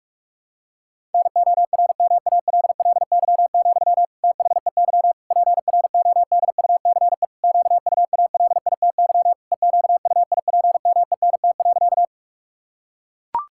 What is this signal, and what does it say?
Morse code, heard as a keyed tone, about 41 wpm, 700 Hz. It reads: NORMALLY, THEY PRODUCE QUALITY EQUIPMENT.
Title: Sentences from Top 1000 Words - Review Part 6 - 45wpm